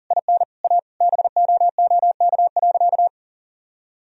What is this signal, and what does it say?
Morse code, heard as a keyed tone, about 40 wpm, 700 Hz. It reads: IN A BOOK.
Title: Sentences from Top 1000 Words - Review Part 1 - 40wpm